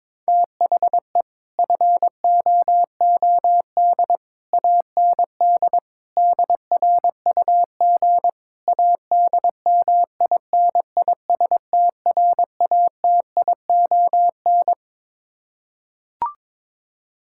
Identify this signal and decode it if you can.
Morse code, heard as a keyed tone, about 22 wpm, 700 Hz. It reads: THE FOOD AND DRUG ADMINISTRATION
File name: practice / 22wpm / Build A Phrase - 5-Word Phrases - 22wpm